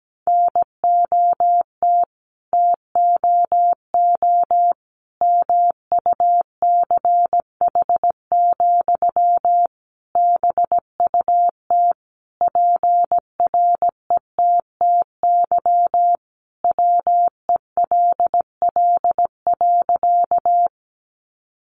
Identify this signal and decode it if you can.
Morse code, heard as a keyed tone, about 17 wpm, 700 Hz. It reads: NOT TOO MUCH, BUT PRETTY WELL.
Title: Sentences from Top 1000 Words 17wpm